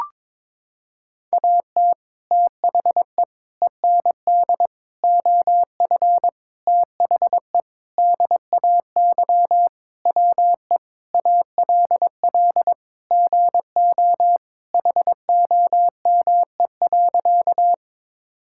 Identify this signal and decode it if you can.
Morse code, heard as a keyed tone, about 22 wpm, 700 Hz. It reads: AT THE END OF THE DAY WE ALL GO HOME.
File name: practice / 22wpm / Sentences with 4-Word Phrases - 22wpm